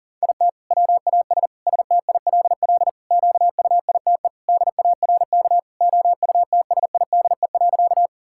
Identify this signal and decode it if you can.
Morse code, heard as a keyed tone, about 40 wpm, 700 Hz. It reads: IT WAS STILL QUITE DARK OUTSIDE.
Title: Sentences from Top 1000 Words - Review Part 6 - 40wpm